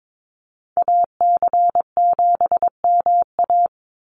Morse code, heard as a keyed tone, about 22 wpm, 700 Hz. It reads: AC7MA
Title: NA SOTA Activator Callsigns 22wpm